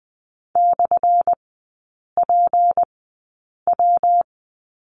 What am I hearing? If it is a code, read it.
Morse code, heard as a keyed tone, about 20 wpm, spaced as for 8 wpm, 700 Hz. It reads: /PW
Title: 178 Lesson 29e - Slash - Three Characters 20wpm